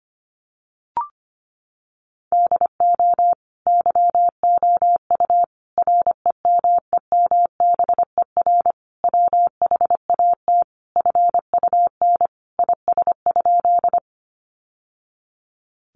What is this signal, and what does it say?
Morse code, heard as a keyed tone, about 25 wpm, 700 Hz. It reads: DO YOU REMEMBER WHAT FUN IS?
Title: Sentences from Top 700 Words 25wpm